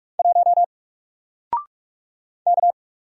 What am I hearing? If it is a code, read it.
Morse code, heard as a keyed tone, about 44 wpm, 700 Hz. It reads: 1 K
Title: Single Character (Letter:Number:Punctuation:Prosign) 45wpm